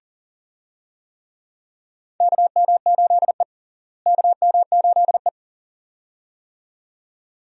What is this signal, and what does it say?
Morse code, heard as a keyed tone, about 37 wpm, 700 Hz. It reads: KM8E KM8E
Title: USA Call Signs - 4 Characters - Repeated 2x - 40wpm